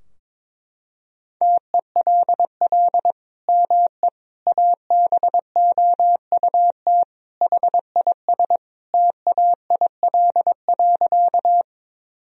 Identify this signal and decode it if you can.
Morse code, heard as a keyed tone, about 22 wpm, 700 Hz. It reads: TELL ME ABOUT HIS TAIL.
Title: Sentences from Top 1000 Words - Review Part 3 - 22wpm